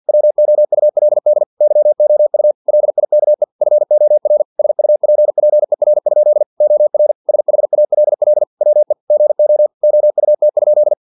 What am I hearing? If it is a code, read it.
Morse code, heard as a keyed tone, about 50 wpm, 600 Hz. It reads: WOULD YOU LIKE FOR SUPPER? OR SHALL WE GO OUT?
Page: Never 49wpm